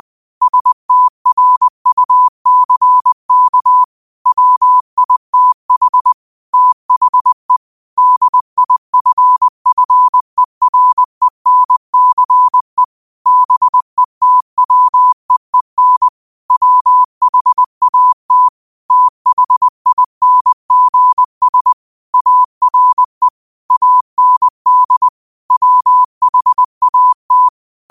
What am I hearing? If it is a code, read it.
Morse code, heard as a keyed tone, about 20 wpm, 1000 Hz. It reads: STRUCK WITH THE DIFFERENCE BETWEEN WHAT THINGS ARE AND WHAT
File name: qotdfiles / 20wpm